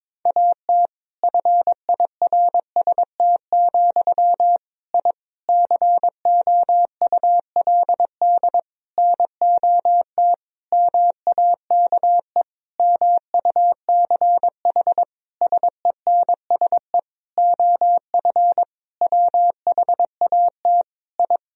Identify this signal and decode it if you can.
Morse code, heard as a keyed tone, about 22 wpm, 700 Hz. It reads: AT FIRST, I COULD NOT MAKE MUCH SENSE OF WHAT I
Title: Sentences from Top 1000 Words - Review Part 4 - 22wpm